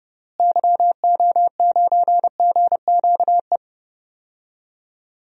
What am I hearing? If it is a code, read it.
Morse code, heard as a keyed tone, about 30 wpm, 700 Hz. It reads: YO9GQE
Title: DXCC-100 Call Signs - 6 Characters - Repeated 2x - 30wpm